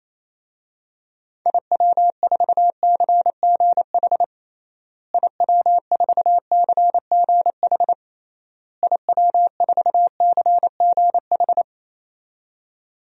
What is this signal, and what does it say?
Morse code, heard as a keyed tone, about 28 wpm, 700 Hz. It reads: IW4CGH IW4CGH IW4CGH
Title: DXCC-100 Call Signs - 6 Characters - Repeated 3x - 28wpm